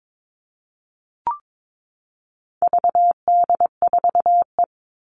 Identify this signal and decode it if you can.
Morse code, heard as a keyed tone, about 22 wpm, 700 Hz. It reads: VD4E
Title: All Cut Numbers - 4-Digit 22wpm